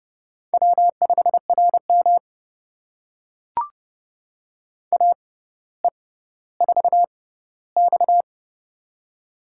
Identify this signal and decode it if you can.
Morse code, heard as a keyed tone, about 29 wpm, spaced as for 15 wpm, 700 Hz. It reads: W5RM AE4X
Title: USA Call Signs - 4 Characters - 6x Character Spacing - 30wpm